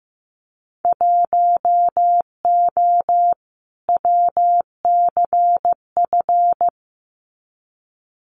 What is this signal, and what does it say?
Morse code, heard as a keyed tone, about 15 wpm, 700 Hz. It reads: 1O WCF
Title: Winter Field Day Exchanges 15wpm